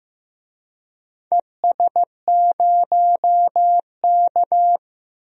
Morse code, heard as a keyed tone, about 15 wpm, 700 Hz. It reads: ES0K
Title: DXCC-100 Call Signs - 4 Characters - 15wpm